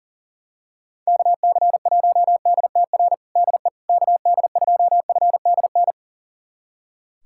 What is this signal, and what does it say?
Morse code, heard as a keyed tone, about 40 wpm, 700 Hz. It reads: KC1DTR DE KD2FDN
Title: USA Callsign DE Callsign 40wpm